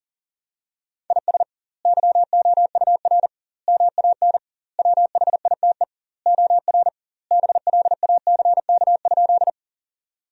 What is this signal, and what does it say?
Morse code, heard as a keyed tone, about 37 wpm, 700 Hz. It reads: IS YOUR MAN WHITE OR BLACK?
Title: 2x Word-Spacing - Sentences from Top 400 Words - 40wpm